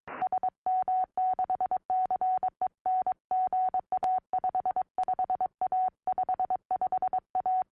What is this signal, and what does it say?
Morse code, heard as a keyed tone, about 22 wpm, 740 Hz. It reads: SM6CEN GA 55A 55A
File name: au / sm6cen 2m